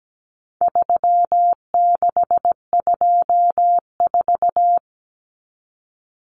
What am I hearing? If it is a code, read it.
Morse code, heard as a keyed tone, about 17 wpm, 700 Hz. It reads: 3624
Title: Common Cut Numbers - 4-Digit 17wpm